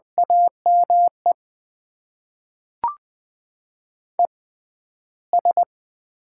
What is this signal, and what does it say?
Morse code, heard as a keyed tone, about 20 wpm, spaced as for 12 wpm, 700 Hz. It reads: AME ES
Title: Top 20 QSO Elements - 6x Character Spacing - 20wpm